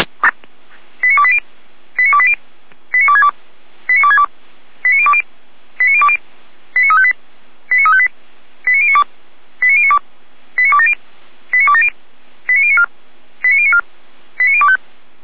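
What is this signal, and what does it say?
Five-tone selective call (ZVEI), heard as quick runs of five tones, tones of about 70 ms. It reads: 89280 89280 89262 89262 89020 89020 89358 89358 89092 89092 89270 89270 89094 89094 89025
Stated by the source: ZVEI five-tone call received by SDR.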